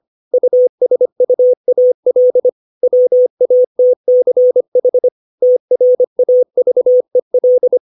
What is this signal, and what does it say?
Morse code, heard as a keyed tone, about 25 wpm, 500 Hz. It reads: USUAL WATCH TRAVEL
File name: 632 words / 632 Words 25WPM